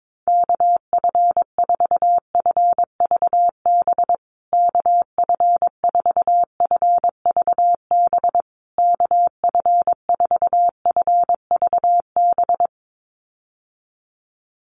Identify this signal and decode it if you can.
Morse code, heard as a keyed tone, about 22 wpm, 700 Hz. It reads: KF4FVB KF4FVB KF4FVB
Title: USA Call Signs Repeated 3x 22wpm